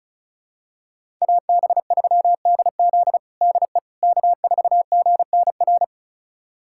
Morse code, heard as a keyed tone, about 36 wpm, 700 Hz. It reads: AB3DZ DE K4GNR
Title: USA Callsign DE Callsign - 5 Characters - 35wpm